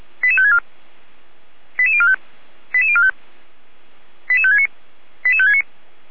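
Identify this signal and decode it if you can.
Five-tone selective call (ZVEI), heard as quick runs of five tones, tones of about 70 ms. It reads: 88564 88045 88045 88569 88569
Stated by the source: Received by SDR; selective call numbers in /455/ folder.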